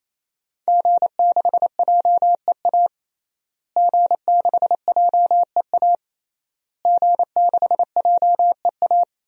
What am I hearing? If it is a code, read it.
Morse code, heard as a keyed tone, about 28 wpm, 700 Hz. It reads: G6JEA G6JEA G6JEA
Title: DXCC-100 Call Signs - 5 Characters - Repeated 3x - 28wpm